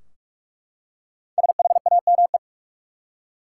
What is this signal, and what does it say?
Morse code, heard as a keyed tone, about 44 wpm, 700 Hz. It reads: SHAME
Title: Five Letter Words - 45wpm